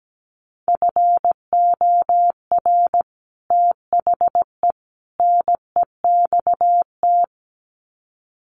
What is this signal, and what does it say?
Morse code, heard as a keyed tone, about 17 wpm, 700 Hz. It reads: FOR THE NEXT
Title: Sets of 3 Words 17wpm